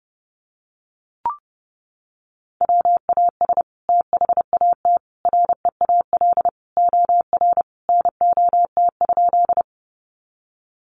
Morse code, heard as a keyed tone, about 30 wpm, 700 Hz. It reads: WAS THAT REAL OR NOT?
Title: Sentences from Top 600 Words 30wpm